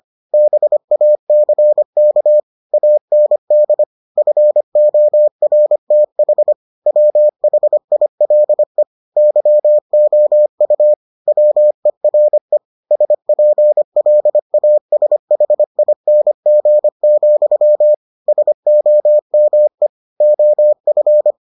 Morse code, heard as a keyed tone, about 25 wpm, 600 Hz. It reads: BACK AND FORTH WHILE YOU WERE SPLASHING, SOME OF